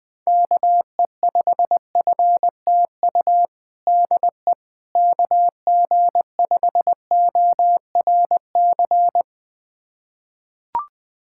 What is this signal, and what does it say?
Morse code, heard as a keyed tone, about 20 wpm, 700 Hz. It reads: KE5FTU DE KG5ORC
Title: USA Callsign DE Callsign 20wpm